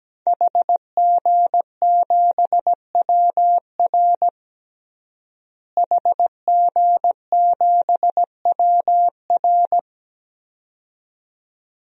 Morse code, heard as a keyed tone, about 17 wpm, 700 Hz. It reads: HG7WR HG7WR
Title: DXCC-100 Call Signs - 5 Characters - Repeated 2x - 17wpm